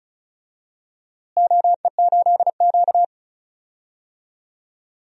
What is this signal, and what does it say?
Morse code, heard as a keyed tone, about 36 wpm, 690 Hz. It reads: OE8Q
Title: DXCC-100 Call Signs - 4 Characters - Repeated 3x - 35wpm